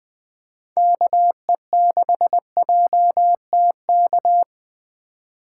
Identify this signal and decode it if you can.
Morse code, heard as a keyed tone, about 20 wpm, 700 Hz. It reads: KE6JTK
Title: USA Call Signs - 6 Characters - 4x Character Spacing - 20wpm